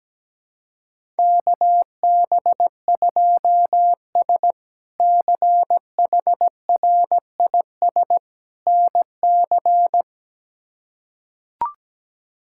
Morse code, heard as a keyed tone, about 17 wpm, 700 Hz. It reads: KB2S CHRIS NC
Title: K1USN SST QSO Practice - 17wpm